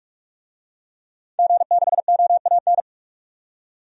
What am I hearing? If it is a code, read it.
Morse code, heard as a keyed tone, about 42 wpm, 700 Hz. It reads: G6OAN
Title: DXCC-100 Call Signs - 5 Characters - 45wpm